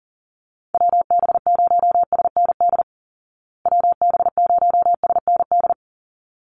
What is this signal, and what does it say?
Morse code, heard as a keyed tone, about 37 wpm, 700 Hz. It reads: WB0SND WB0SND
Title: Call Signs Repeated 3x - 40wpm